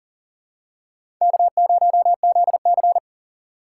Morse code, heard as a keyed tone, about 40 wpm, 700 Hz. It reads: K0ZC
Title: USA Call Signs - 6x Character Spacing - 40wpm